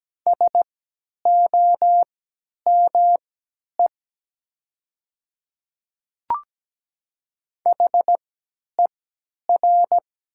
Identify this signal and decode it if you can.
Morse code, heard as a keyed tone, about 17 wpm, spaced as for 10 wpm, 700 Hz. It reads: SOME HER
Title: 3x Character Spacing - Top 100 Words - 17wpm